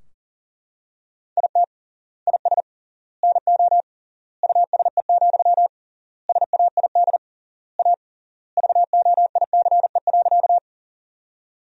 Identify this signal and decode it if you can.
Morse code, heard as a keyed tone, about 40 wpm, 700 Hz. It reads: IT IS NO USE, SAID A VOICE.
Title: 3x Word-Spacing - Sentences from Top 500 Words - 40wpm